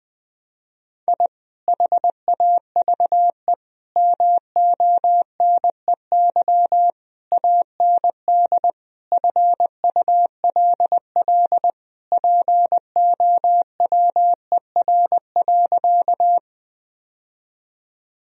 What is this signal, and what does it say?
Morse code, heard as a keyed tone, about 20 wpm, 700 Hz. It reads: I HAVE MONEY AND FULL POWER.
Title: Sentences from Top 1000 Words - Review Part 3 - 20wpm